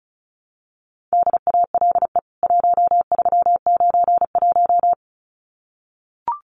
Morse code, heard as a keyed tone, about 34 wpm, 700 Hz. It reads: DALE 1391
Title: CWT Exchange 35wpm